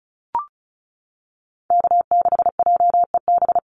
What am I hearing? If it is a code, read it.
Morse code, heard as a keyed tone, about 34 wpm, 700 Hz. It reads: K6JEB
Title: Call Signs - 35wpm